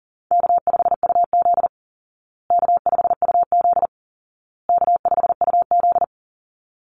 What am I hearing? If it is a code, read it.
Morse code, heard as a keyed tone, about 40 wpm, 700 Hz. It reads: K5UZ K5UZ K5UZ
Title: Call Signs Repeated 2x - 40wpm